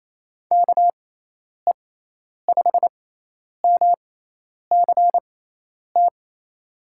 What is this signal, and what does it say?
Morse code, heard as a keyed tone, about 28 wpm, spaced as for 10 wpm, 700 Hz. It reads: KE5MCT
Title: USA Call Signs - 6 Characters - 6x Character Spacing - 28wpm